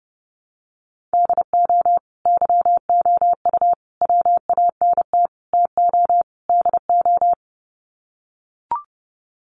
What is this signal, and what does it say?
Morse code, heard as a keyed tone, about 30 wpm, 700 Hz. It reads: DO YOU WANT TO DO
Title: Sets of 5 Words Encore 30wpm